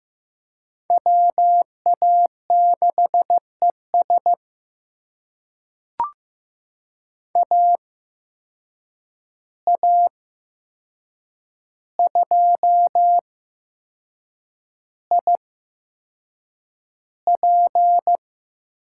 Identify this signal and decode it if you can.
Morse code, heard as a keyed tone, about 15 wpm, spaced as for 6 wpm, 700 Hz. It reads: WA6ES AA2IP
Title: USA Call Signs - 5 Characters - 8x Character Spacing - 15wpm